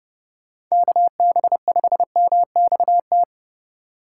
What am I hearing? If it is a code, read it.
Morse code, heard as a keyed tone, about 34 wpm, 700 Hz. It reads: KB5MXT